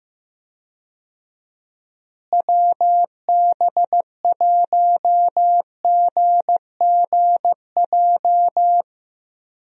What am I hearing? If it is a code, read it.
Morse code, heard as a keyed tone, about 15 wpm, 700 Hz. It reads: WB1GGJ
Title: USA Call Signs - 6 Characters - Repeated 2x - 15wpm